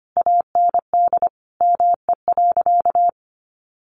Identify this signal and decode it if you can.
Morse code, heard as a keyed tone, about 25 wpm, 700 Hz. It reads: AND ME.
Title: Sentences from Top 1200 Words 25wpm